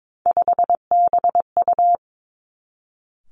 Morse code, heard as a keyed tone, about 22 wpm, 700 Hz. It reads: 5BU
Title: Three Challenging Letter-Number Combinations 22wpm